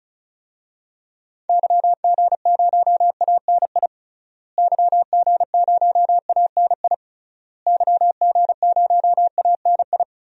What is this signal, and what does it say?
Morse code, heard as a keyed tone, about 34 wpm, 700 Hz. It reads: YG0ANI YG0ANI YG0ANI
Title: DXCC-100 Call Signs - 6 Characters - Repeated 3x - 35wpm